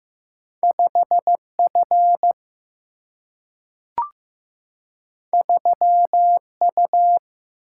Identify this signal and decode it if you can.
Morse code, heard as a keyed tone, about 15 wpm, 700 Hz. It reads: 5F 3U
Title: Single-Letter Single-Number 15wpm